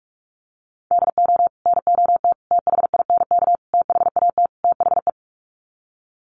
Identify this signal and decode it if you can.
Morse code, heard as a keyed tone, about 44 wpm, 700 Hz. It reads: DO NOT THINK THAT THE
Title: Sets of 5 Words Encore2 45wpm